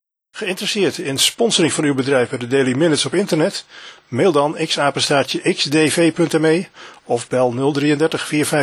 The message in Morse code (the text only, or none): none